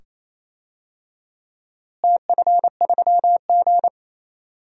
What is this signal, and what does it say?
Morse code, heard as a keyed tone, about 28 wpm, 700 Hz. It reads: TF3G
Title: DXCC-100 Call Signs - 4 Characters - Repeated 2x - 28wpm